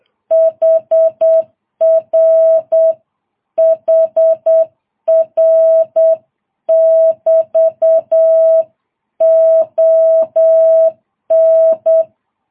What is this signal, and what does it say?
Morse code, heard as a keyed tone, about 8 wpm, 650 Hz. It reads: HR HR <BT> ON
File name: jh8cbh